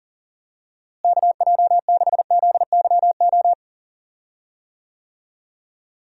Morse code, heard as a keyed tone, about 40 wpm, 700 Hz. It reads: KJ6ZYO
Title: USA Call Signs - 6 Characters - Repeated 3x - 40wpm